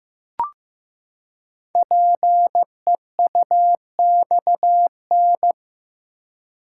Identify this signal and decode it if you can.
Morse code, heard as a keyed tone, about 15 wpm, 700 Hz. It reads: PEUXN
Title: Code Groups - 5 Letters - 15wpm